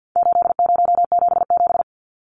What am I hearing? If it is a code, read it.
Morse code, heard as a keyed tone, about 50 wpm, 700 Hz. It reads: 8077
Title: Numbers - 4-Digit 50wpm